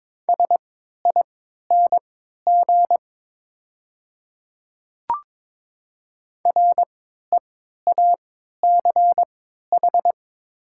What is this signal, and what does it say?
Morse code, heard as a keyed tone, about 22 wpm, spaced as for 12 wpm, 700 Hz. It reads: SING REACH
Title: 3x Character Spacing - Top 500 Words - 22wpm